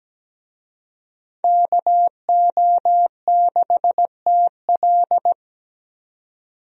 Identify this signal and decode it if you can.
Morse code, heard as a keyed tone, about 17 wpm, 700 Hz. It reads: KO6TL
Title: USA Call Signs - 5 Characters - Repeated 2x - 17wpm